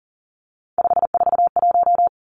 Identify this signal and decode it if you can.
Morse code, heard as a keyed tone, about 39 wpm, 700 Hz. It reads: 541
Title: Three Challenging Letter-Number Combinations 40wpm